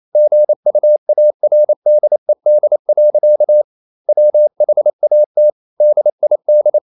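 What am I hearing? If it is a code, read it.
Morse code, heard as a keyed tone, about 28 wpm, 600 Hz. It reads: GUARDED. WHAT DID
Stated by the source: Morse code practice at 28 wpm, text of Ken Follett's novel Never in English, from CW med Gnister.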